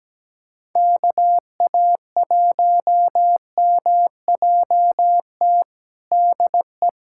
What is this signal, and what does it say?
Morse code, heard as a keyed tone, about 17 wpm, 700 Hz. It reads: KA1MJT DE
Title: USA Callsign DE Callsign - 6 Characters - 17wpm